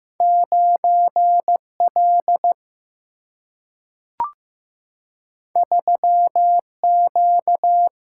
Morse code, heard as a keyed tone, about 15 wpm, 700 Hz. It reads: 9L 3Q
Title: Single-Letter Single-Number 15wpm